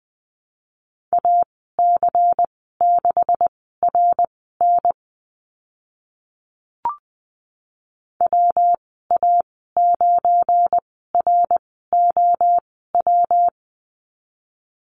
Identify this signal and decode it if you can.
Morse code, heard as a keyed tone, about 20 wpm, spaced as for 15 wpm, 700 Hz. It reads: AC6RN WA9ROW
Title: 173 Lesson 28g - Number 9 - Call Signs 20wpm